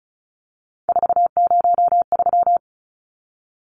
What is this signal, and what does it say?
Morse code, heard as a keyed tone, about 34 wpm, 700 Hz. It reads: 403
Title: Numbers - 3-Digit 35wpm